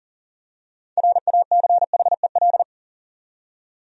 Morse code, heard as a keyed tone, about 38 wpm, 700 Hz. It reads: RACHEL
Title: USA Common Names - 40wpm